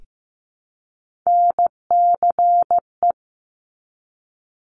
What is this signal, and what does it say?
Morse code, heard as a keyed tone, about 15 wpm, 700 Hz. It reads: NCE
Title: Top Three Letter Combinations in Words 15wpm